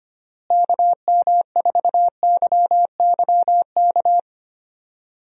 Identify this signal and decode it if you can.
Morse code, heard as a keyed tone, about 25 wpm, 700 Hz. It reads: KM4YYK
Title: USA Call Signs - 6 Characters - Repeated 2x - 25wpm